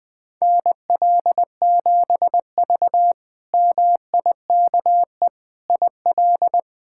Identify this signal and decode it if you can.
Morse code, heard as a keyed tone, about 21 wpm, 700 Hz. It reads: NL7V MIKE IL
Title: K1USN SST QSO Practice - 20wpm